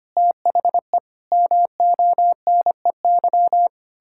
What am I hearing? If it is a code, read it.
Morse code, heard as a keyed tone, about 25 wpm, 700 Hz. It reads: THE MONEY
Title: Pangram Challenge v2 - 25wpm